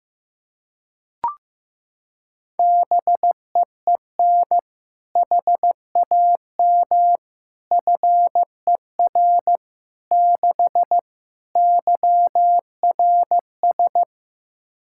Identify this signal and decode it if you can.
Morse code, heard as a keyed tone, about 15 wpm, 700 Hz. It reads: BEEN HAM FER 6 YRS
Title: K1USN SST QSO Practice 15wpm